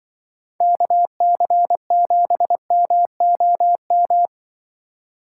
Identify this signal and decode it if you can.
Morse code, heard as a keyed tone, about 23 wpm, 700 Hz. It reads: KC7MOM
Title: Speed Racing - USA 2x3 Call Signs 20wpm